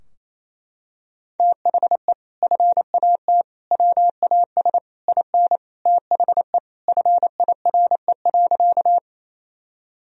Morse code, heard as a keyed tone, about 28 wpm, 700 Hz. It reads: THE FAT WAS IN THE FIRE.
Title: Sentences from Top 1000 Words - Review Part 4 - 28wpm